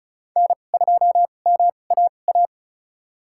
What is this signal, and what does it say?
Morse code, heard as a keyed tone, about 34 wpm, spaced as for 25 wpm, 700 Hz. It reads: N2MAA
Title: USA Call Signs - 2x Character Spacing - 35wpm